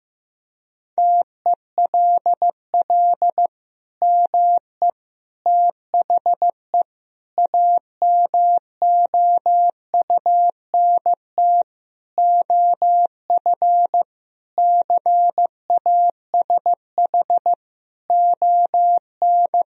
Morse code, heard as a keyed tone, about 16 wpm, 700 Hz. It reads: TELL ME THE AMOUNT OF CASH ON
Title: Sentences from Top 1000 Words - Review Part 4 - 15wpm